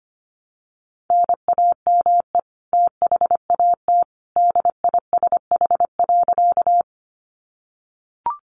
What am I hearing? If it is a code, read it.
Morse code, heard as a keyed tone, about 25 wpm, 700 Hz. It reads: NAME THAT DISH.
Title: Sentences from Top 1300 Words 25wpm